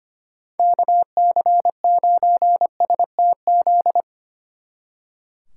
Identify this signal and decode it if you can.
Morse code, heard as a keyed tone, about 31 wpm, 700 Hz. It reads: KC9STZ
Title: Speed Racing - USA 2x3 Call Signs 25wpm